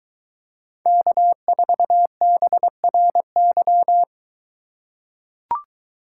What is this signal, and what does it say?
Morse code, heard as a keyed tone, about 21 wpm, 700 Hz. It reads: K4BRY